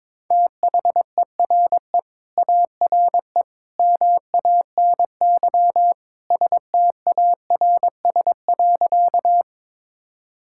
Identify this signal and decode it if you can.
Morse code, heard as a keyed tone, about 22 wpm, 700 Hz. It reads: THERE ARE MANY STARS.
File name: practice / 22wpm / Sentences with 2-Word Phrases - 22wpm